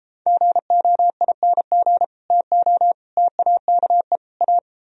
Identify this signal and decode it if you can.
Morse code, heard as a keyed tone, about 34 wpm, 700 Hz. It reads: GOING TO TAKE A